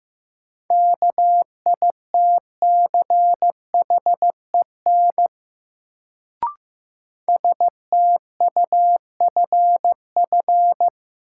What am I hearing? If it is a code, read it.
Morse code, heard as a keyed tone, about 15 wpm, 700 Hz. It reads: KITCHEN STUFF
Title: Sentences from Top 1400 Words - New Word Drill - 15wpm